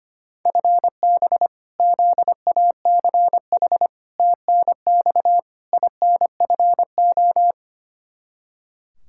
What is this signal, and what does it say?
Morse code, heard as a keyed tone, about 25 wpm, 700 Hz. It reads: FB ZACH TNX INFO
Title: K1USN SST QSO Practice 25wpm